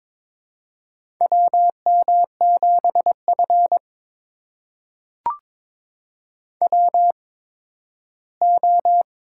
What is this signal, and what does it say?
Morse code, heard as a keyed tone, about 21 wpm, spaced as for 13 wpm, 700 Hz. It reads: WM7F WO
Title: USA Call Signs - 4 Characters - 8x Character Spacing - 22wpm